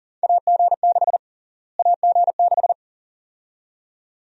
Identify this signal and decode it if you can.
Morse code, heard as a keyed tone, about 40 wpm, 700 Hz. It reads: AG6 AG6
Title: USA Call Sign Prefixes - Repeated 2x - 40wpm